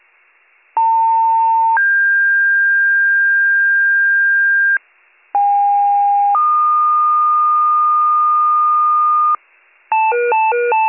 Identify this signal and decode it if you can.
Two-tone page, heard as a one-second tone, then a three-second tone, about 900 then 1600 Hz, about 810 then 1220 Hz, with noise. It is heard from 0.8 to 9.4 s.